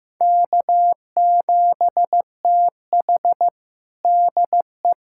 Morse code, heard as a keyed tone, about 15 wpm, 700 Hz. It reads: K7TH DE
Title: USA Callsign DE Callsign - 4 Characters - 15wpm